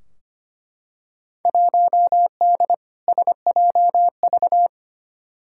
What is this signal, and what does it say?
Morse code, heard as a keyed tone, about 25 wpm, 700 Hz. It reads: 1D SJV
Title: Field Day Exchanges Encore - 25wpm